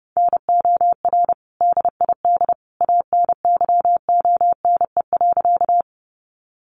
Morse code, heard as a keyed tone, about 30 wpm, 700 Hz. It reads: NOR DID ANYONE.
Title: Sentences from Top 1500 Words 30wpm